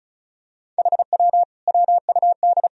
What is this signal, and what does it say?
Morse code, heard as a keyed tone, about 36 wpm, 700 Hz. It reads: HW WUD
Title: Sentences from Top 100 Words and Abbreviations 35wpm